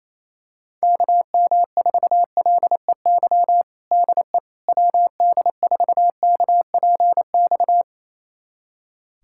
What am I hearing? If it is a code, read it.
Morse code, heard as a keyed tone, about 28 wpm, 700 Hz. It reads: KM4LEY DE WD4KPX
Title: USA Callsign DE Callsign 28wpm